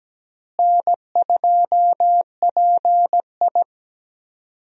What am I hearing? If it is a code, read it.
Morse code, heard as a keyed tone, about 17 wpm, 700 Hz. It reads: N2PI
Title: USA Call Signs - 4 Characters - Repeated 3x - 17wpm